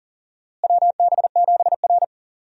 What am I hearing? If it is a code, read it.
Morse code, heard as a keyed tone, about 40 wpm, 700 Hz. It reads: WB7R
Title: USA Call Signs - 4 Characters - Repeated 2x - 40wpm